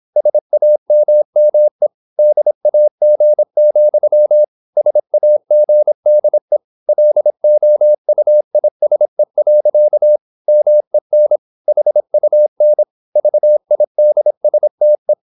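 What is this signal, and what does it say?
Morse code, heard as a keyed tone, about 26 wpm, 600 Hz. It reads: SAMME DAG, SAGDE LOUISE. MEN HUN VIDSTE
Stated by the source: High-speed (QRQ) Morse at 26 wpm.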